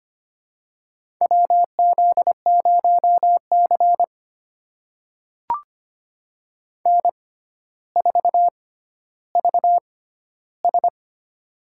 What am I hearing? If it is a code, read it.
Morse code, heard as a keyed tone, about 25 wpm, spaced as for 13 wpm, 700 Hz. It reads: WZ0C N4VS
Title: USA Call Signs - 4 Characters - 6x Character Spacing - 25wpm